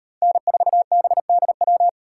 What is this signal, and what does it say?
Morse code, heard as a keyed tone, about 36 wpm, 700 Hz. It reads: N4BDW